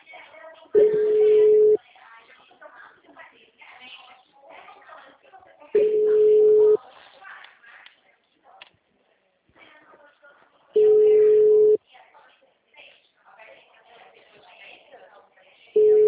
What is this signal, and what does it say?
Ringback tone (Europe, 425 Hz), with speech.
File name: out- -2012- - - .64